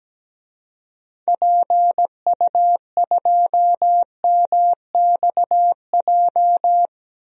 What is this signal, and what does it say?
Morse code, heard as a keyed tone, about 17 wpm, 700 Hz. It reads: PU2MXJ
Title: DXCC-100 Call Signs - 6 Characters - Repeated 2x - 17wpm